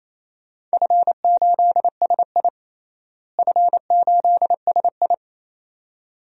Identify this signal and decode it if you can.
Morse code, heard as a keyed tone, about 28 wpm, 700 Hz. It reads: F8SI F8SI
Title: DXCC-100 Call Signs - 4 Characters - Repeated 2x - 28wpm